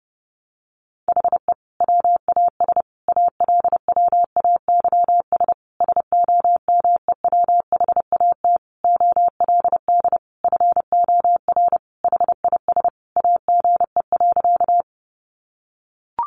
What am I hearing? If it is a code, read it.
Morse code, heard as a keyed tone, about 30 wpm, 700 Hz. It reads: HE WAS ALWAYS SOMEWHAT OLD FOR HIS AGE.
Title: Sentences from Top 1300 Words 30wpm